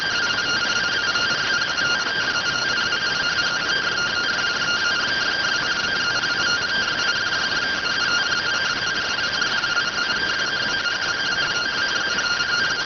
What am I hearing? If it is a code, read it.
Two-tone teleprinter signal, about 170 Hz shift.